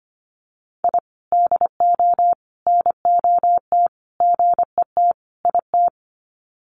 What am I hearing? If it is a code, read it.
Morse code, heard as a keyed tone, about 25 wpm, 700 Hz. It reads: I DO NOT GET IT
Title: Sets of 5 Words Encore 25wpm